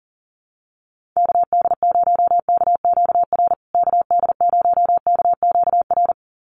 Morse code, heard as a keyed tone, about 40 wpm, 700 Hz. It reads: KD0KQR KD0KQR
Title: USA Call Signs Repeated 2x 40wpm